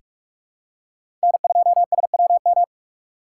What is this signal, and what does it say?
Morse code, heard as a keyed tone, about 43 wpm, 700 Hz. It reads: N2SWM
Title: USA Call Signs - 5 Characters - 2x Character Spacing - 45wpm